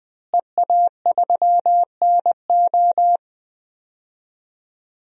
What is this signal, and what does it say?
Morse code, heard as a keyed tone, about 21 wpm, 700 Hz. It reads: EA3NO
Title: DXCC-100 Call Signs - 5 Characters - Repeated 2x - 20wpm